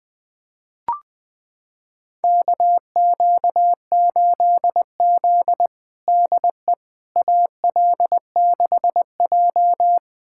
Morse code, heard as a keyed tone, about 20 wpm, 700 Hz. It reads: KQ8Z DE AL6J
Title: USA Callsign DE Callsign - 4 Characters - 20wpm